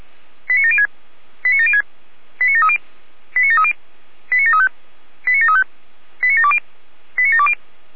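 Five-tone selective call (ZVEI), heard as quick runs of five tones, tones of about 70 ms. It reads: 89706 89706 89730 89730 89735 89735 89720 89720
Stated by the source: ZVEI five-tone selective call 89752 received by SDR.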